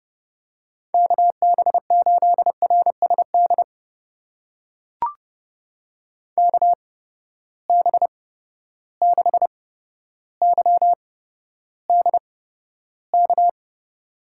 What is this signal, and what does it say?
Morse code, heard as a keyed tone, about 30 wpm, spaced as for 13 wpm, 700 Hz. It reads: KB8RSD KB6YDK
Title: USA Call Signs - 6 Characters - 8x Character Spacing - 30wpm